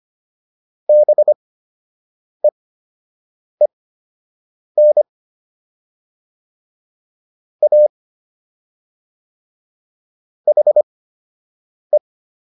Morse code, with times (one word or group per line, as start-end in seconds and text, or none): 0.9-5.0 BEEN
7.6-7.9 A
10.5-12.0 HE